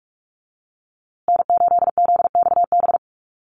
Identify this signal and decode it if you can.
Morse code, heard as a keyed tone, about 44 wpm, 700 Hz. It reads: N8ZXB
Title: USA Call Signs Repeated 3x 45wpm